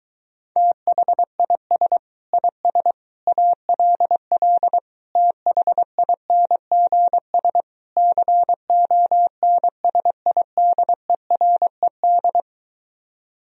Morse code, heard as a keyed tone, about 23 wpm, 700 Hz. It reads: THIS IS ALL THINGS CONSIDERED